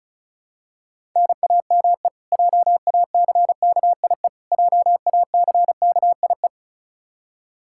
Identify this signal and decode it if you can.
Morse code, heard as a keyed tone, about 36 wpm, 700 Hz. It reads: NAME JACKIE JACKIE
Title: Practice QSO - Exchange 1 - Beginning - 35wpm